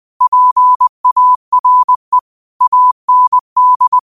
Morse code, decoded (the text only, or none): PARE AND